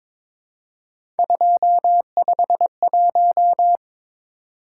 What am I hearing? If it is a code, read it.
Morse code, heard as a keyed tone, about 22 wpm, 700 Hz. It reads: 251
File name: practice / 22wpm / Three Letter-Number Combinations - 22wpm